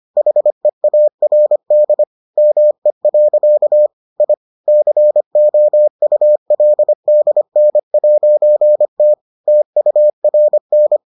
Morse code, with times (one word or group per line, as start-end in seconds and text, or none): 0.2-2.0 HEARD
2.4-3.9 ME.
4.2-4.3 I
4.7-9.1 COULDN'T
9.5-11.0 TURN